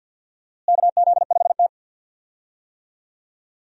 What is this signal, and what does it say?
Morse code, heard as a keyed tone, about 52 wpm, 700 Hz. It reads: KZ5T
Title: USA Call Signs - 4 Characters - Repeated 2x - 50wpm